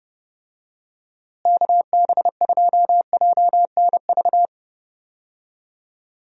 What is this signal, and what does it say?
Morse code, heard as a keyed tone, about 32 wpm, 700 Hz. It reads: KB2JNV
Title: USA Call Signs - 6 Characters - Repeated 3x - 30wpm